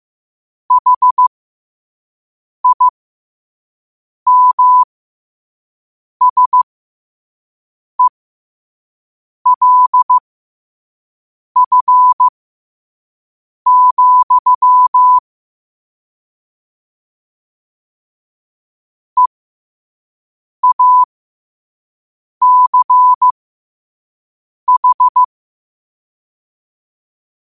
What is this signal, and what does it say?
Morse code, heard as a keyed tone, about 15 wpm, spaced as for 5 wpm, 1000 Hz. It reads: HIMSELF, EACH
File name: qotdfiles / 5wpm